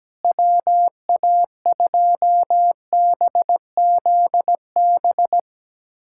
Morse code, heard as a keyed tone, about 17 wpm, 700 Hz. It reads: WA2BZB
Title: USA Callsign DE Callsign - 6 Characters - 17wpm